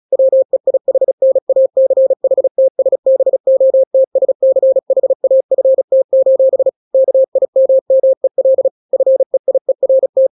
Morse code, heard as a keyed tone, about 36 wpm, 530 Hz. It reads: WEIHNACHTSBOTSCHAFT: KIMMEL FEIERT
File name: daily / 35wpm